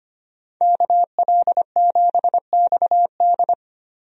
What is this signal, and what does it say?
Morse code, heard as a keyed tone, about 25 wpm, 700 Hz. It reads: KL7XD
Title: DXCC-100 Call Signs - 5 Characters - 25wpm